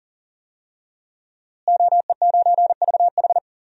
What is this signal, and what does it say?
Morse code, heard as a keyed tone, about 40 wpm, 700 Hz. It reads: OE9VH
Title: DXCC-100 Call Signs - 5 Characters - Repeated 3x - 40wpm